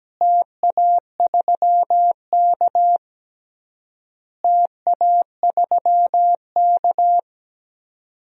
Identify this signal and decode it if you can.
Morse code, heard as a keyed tone, about 17 wpm, 700 Hz. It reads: TA3K TA3K
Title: DXCC-100 Call Signs - 4 Characters - Repeated 3x - 17wpm